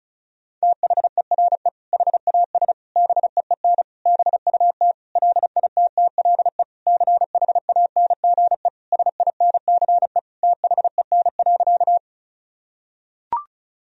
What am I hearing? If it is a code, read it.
Morse code, heard as a keyed tone, about 36 wpm, 700 Hz. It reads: THERE HAS BEEN BUT LITTLE CHANGE SINCE THEN.
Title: Sentences from Top 1000 Words - Review Part 1 - 35wpm